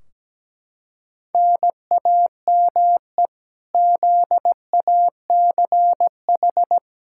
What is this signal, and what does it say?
Morse code, heard as a keyed tone, about 17 wpm, 700 Hz. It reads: NAME ZACH
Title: Practice QSO - Exchange 1 - Beginning - 17wpm